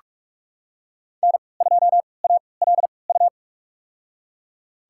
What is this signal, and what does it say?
Morse code, heard as a keyed tone, about 46 wpm, spaced as for 25 wpm, 700 Hz. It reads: N2ALU
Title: USA Call Signs - 3x Character Spacing - 45wpm